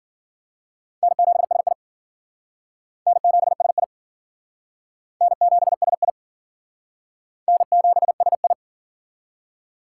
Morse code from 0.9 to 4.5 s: N7SI N7SI

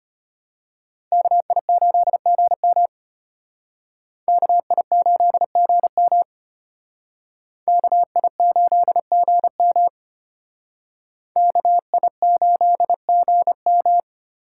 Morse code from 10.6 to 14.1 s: KI8GM

